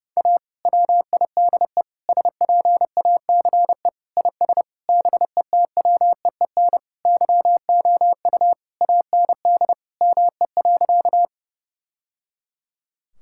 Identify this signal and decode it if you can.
Morse code, heard as a keyed tone, about 32 wpm, 700 Hz. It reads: A WIDE SPACE IS BETWEEN YOU AND ME.
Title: Sentences from Top 1000 Words - Review Part 6 - 30wpm